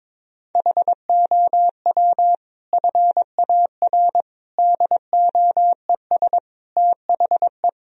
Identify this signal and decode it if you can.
Morse code, heard as a keyed tone, about 22 wpm, 700 Hz. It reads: HOW FAR DOES THE
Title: Sentences from Top 1000 Words - Review Part 4 - 22wpm